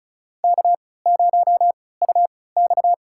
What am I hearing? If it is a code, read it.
Morse code, heard as a keyed tone, about 36 wpm, spaced as for 20 wpm, 700 Hz. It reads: K0UX
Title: USA Call Signs - 3x Character Spacing - 35wpm